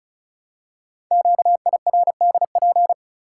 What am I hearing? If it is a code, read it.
Morse code, heard as a keyed tone, about 36 wpm, 690 Hz. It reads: QIRDP
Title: Code Groups - 5 Letter:Numbers - 35wpm